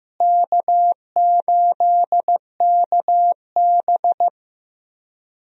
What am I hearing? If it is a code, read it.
Morse code, heard as a keyed tone, about 15 wpm, 700 Hz. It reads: K8KB
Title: USA Call Signs - 4 Characters - Repeated 3x - 15wpm